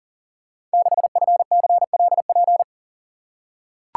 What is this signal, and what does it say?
Morse code, heard as a keyed tone, about 42 wpm, 690 Hz. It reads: 6FCLP
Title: Code Groups - 5 Letter:Numbers - 40wpm